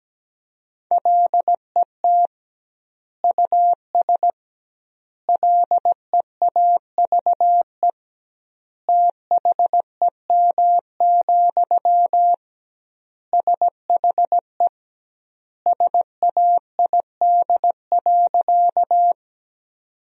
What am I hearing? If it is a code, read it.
Morse code, heard as a keyed tone, about 17 wpm, 700 Hz. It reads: LET US LEAVE THEM, SHE SAID.
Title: 2x Word-Spacing - Sentences from Top 400 Words - 17wpm